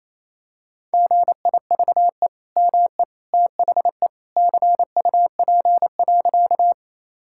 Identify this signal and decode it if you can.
Morse code, heard as a keyed tone, about 28 wpm, 700 Hz. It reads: GIVE ME THE CUP.
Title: Sentences from Top 1600 Words 28wpm